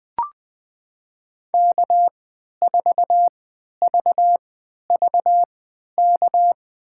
Morse code, heard as a keyed tone, about 20 wpm, spaced as for 11 wpm, 700 Hz. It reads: K4VVK
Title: USA Call Signs - 5 Characters - 3x Character Spacing - 20wpm